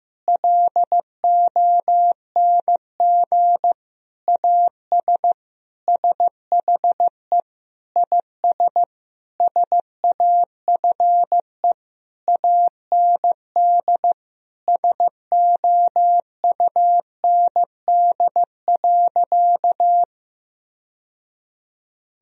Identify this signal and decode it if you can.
Morse code, heard as a keyed tone, about 15 wpm, 700 Hz. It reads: LONG AS SHE IS SAFE AND SOUND.
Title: Sentences from Top 1000 Words - Review Part 1 - 15wpm